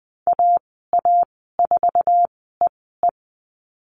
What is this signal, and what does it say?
Morse code, heard as a keyed tone, about 20 wpm, spaced as for 15 wpm, 700 Hz. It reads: AA4EE
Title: 048 Lesson 9g - Number 4 - Call Signs 20wpm